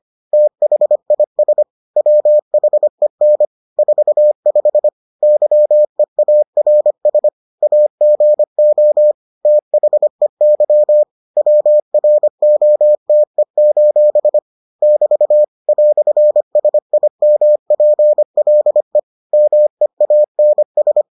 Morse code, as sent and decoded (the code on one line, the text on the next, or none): - .... .. ... / .-- .... -. / ....- ..... / -.-- . .- .-. ... / .- --. --- / - .... -.-- / .-- .-. --- - . ---... / -...- / .-..-. ... .. -- .--. .-.. / -- . .- -. ...
THIS WHEN 45 YEARS AGO THEY WROTE: <BT> "SIMPLE MEANS